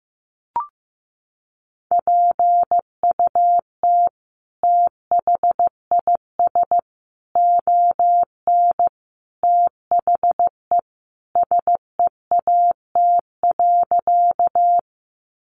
Morse code, heard as a keyed tone, about 15 wpm, 700 Hz. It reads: PUT THIS ON THE SEAT.
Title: Sentences from Top 1500 Words 15wpm